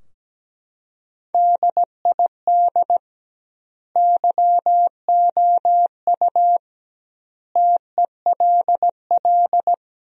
Morse code, heard as a keyed tone, about 17 wpm, 700 Hz. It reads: DID YOU TELL
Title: 2x Word-Spacing - Sentences from Top 400 Words - 17wpm